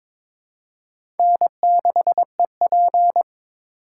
Morse code, heard as a keyed tone, about 22 wpm, 700 Hz. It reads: N6EP
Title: USA Call Signs - 2x Character Spacing - 22wpm